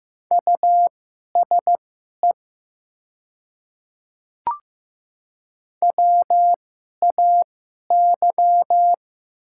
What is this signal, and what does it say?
Morse code, heard as a keyed tone, about 15 wpm, spaced as for 11 wpm, 700 Hz. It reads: USE WAY
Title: 2x Character Spacing - Top 100 Words - 15wpm